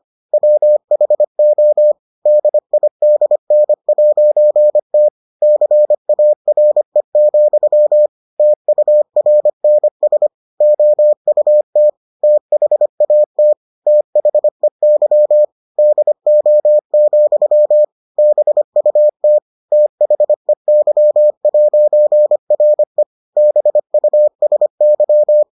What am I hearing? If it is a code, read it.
Morse code, heard as a keyed tone, about 25 wpm, 600 Hz. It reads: WHO DIDN'T CARE, TURNS OUT THAT THEY DO, BUT THEY'RE BUSY